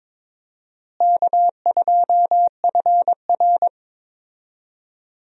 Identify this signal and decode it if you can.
Morse code, heard as a keyed tone, about 22 wpm, 700 Hz. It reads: K2FR
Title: USA Call Signs - 4 Characters - Repeated 2x - 22wpm